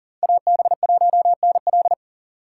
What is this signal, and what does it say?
Morse code, heard as a keyed tone, about 40 wpm, 700 Hz. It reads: AB1NL
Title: USA Call Signs - 6x Character Spacing - 40wpm